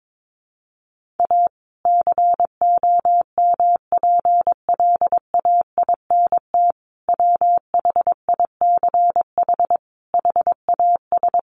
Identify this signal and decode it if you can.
Morse code, heard as a keyed tone, about 22 wpm, 700 Hz. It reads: A COMPLAINT WHICH HAS
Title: Sentences from Top 1400 Words 22wpm